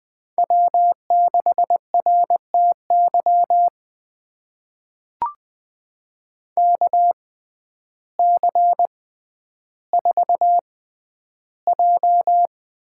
Morse code, heard as a keyed tone, about 20 wpm, spaced as for 11 wpm, 700 Hz. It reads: W6RTY KC4J
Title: USA Call Signs - 6x Character Spacing - 20wpm